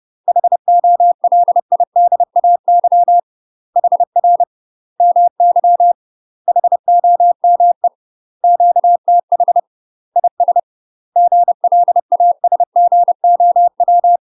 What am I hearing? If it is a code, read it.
Morse code, heard as a keyed tone, about 30 wpm, 700 Hz. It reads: HOLIDAY HR MY HOME QTH IS GLASGOW
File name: UK MASTER QSOs TRAINING / 30WPM / UK MASTER QSOs 0049